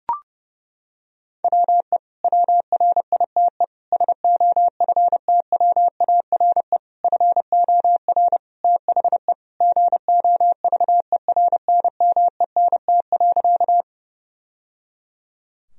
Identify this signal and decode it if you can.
Morse code, heard as a keyed tone, about 32 wpm, 700 Hz. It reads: WE WRITE SOFTWARE FOR THE GOVERNMENT.
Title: Sentences from Top 1000 Words - Review Part 3 - 30wpm